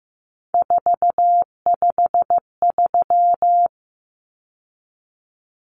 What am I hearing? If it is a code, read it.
Morse code, heard as a keyed tone, about 15 wpm, 700 Hz. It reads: 453
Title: Common Cut Numbers - 3-Digit 15wpm